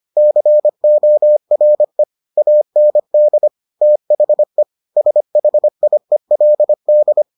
Morse code, heard as a keyed tone, about 25 wpm, 600 Hz. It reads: CORE AND THE SHIELD